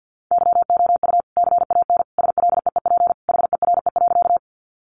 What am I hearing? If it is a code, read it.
Morse code, heard as a keyed tone, about 50 wpm, 700 Hz. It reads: YOU CAN SLEEP HERE.
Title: Sentences from Top 500 Words 50wpm